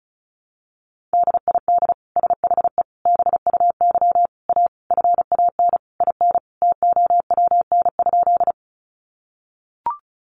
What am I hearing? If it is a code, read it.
Morse code, heard as a keyed tone, about 34 wpm, 700 Hz. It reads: DID SHE BUY A FAN IN TOWN?
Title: Sentences from Top 1300 Words 35wpm